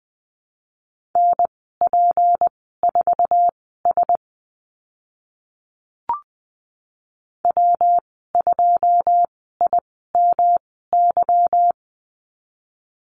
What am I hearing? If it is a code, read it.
Morse code, heard as a keyed tone, about 20 wpm, spaced as for 15 wpm, 700 Hz. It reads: NP4S W2IMY
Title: 154 Lesson 25g - Letter p - Call Signs 20wpm